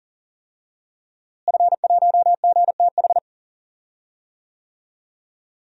Morse code, heard as a keyed tone, about 40 wpm, 700 Hz. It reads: F1GTH
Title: DXCC-100 Call Signs - 5 Characters - Repeated 3x - 40wpm